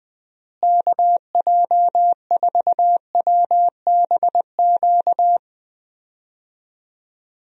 Morse code, heard as a keyed tone, about 21 wpm, 700 Hz. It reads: KJ4WBQ
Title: USA Call Signs - 6 Characters - Repeated 2x - 20wpm